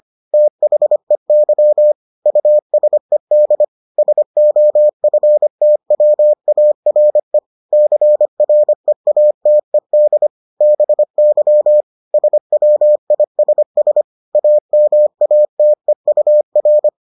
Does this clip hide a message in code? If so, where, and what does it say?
Yes: Morse code from 0.3 to 16.9 s, heard as a keyed tone, about 25 wpm, 600 Hz. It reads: THEY USED SOFTWARE CREATED BY SWISS AMATEUR